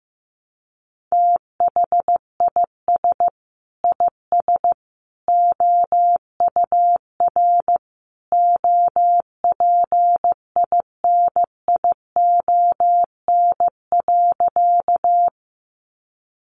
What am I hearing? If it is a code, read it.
Morse code, heard as a keyed tone, about 14 wpm, 700 Hz. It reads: THIS IS OUR OPINION.
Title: Sentences from Top 1200 Words 15wpm